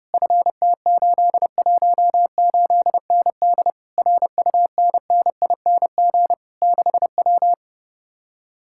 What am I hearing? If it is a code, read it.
Morse code, heard as a keyed tone, about 33 wpm, 700 Hz. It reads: FT818ND RUNNING 6W